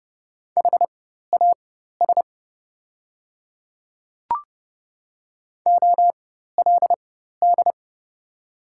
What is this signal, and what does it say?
Morse code, heard as a keyed tone, about 30 wpm, spaced as for 14 wpm, 700 Hz. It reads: HAS OLD
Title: 4x Character Spacing - Top 300 Words - 30wpm